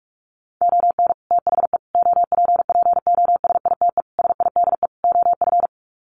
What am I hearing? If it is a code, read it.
Morse code, heard as a keyed tone, about 44 wpm, 700 Hz. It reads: ON THE OPPOSITE SIDE OF
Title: Sets of 5 Words Encore2 45wpm